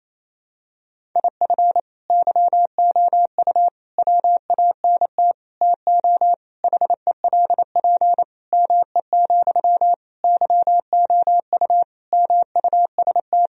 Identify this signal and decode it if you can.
Morse code, heard as a keyed tone, about 28 wpm, 700 Hz. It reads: IF YOU WANT TO HELP ME, YOU MUST
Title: Sentences from Top 1000 Words - Review Part 1 - 28wpm